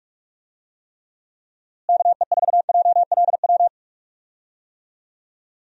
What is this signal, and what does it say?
Morse code, heard as a keyed tone, about 44 wpm, 700 Hz. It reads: KE4JLW
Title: USA Call Signs - 6 Characters - Repeated 2x - 45wpm